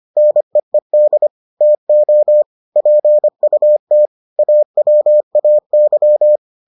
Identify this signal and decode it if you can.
Morse code, heard as a keyed tone, about 25 wpm, 600 Hz. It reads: NEED TO PUT AWAY